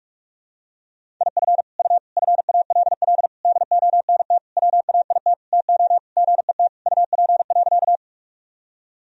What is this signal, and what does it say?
Morse code, heard as a keyed tone, about 41 wpm, 700 Hz. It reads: IF U FALL DONT WAIT TO GET UP.
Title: Sentences from Top 500 Words and Abbreviations 45wpm